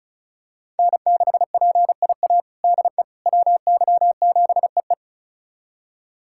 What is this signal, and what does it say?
Morse code, heard as a keyed tone, about 36 wpm, 700 Hz. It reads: N6PIA DE WY7EE
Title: USA Callsign DE Callsign - 5 Characters - 35wpm